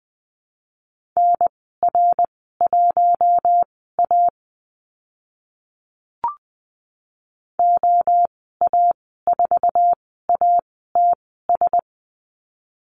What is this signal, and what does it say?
Morse code, heard as a keyed tone, about 20 wpm, spaced as for 15 wpm, 700 Hz. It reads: NR1A OA4ATS
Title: 062 Lesson 11g - Letter h - Call Signs 20wpm